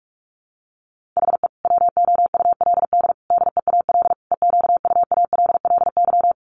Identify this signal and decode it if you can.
Morse code, heard as a keyed tone, about 44 wpm, 700 Hz. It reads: HE WOULD DEAL EQUALLY
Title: Sentences from Top 1400 Words 45wpm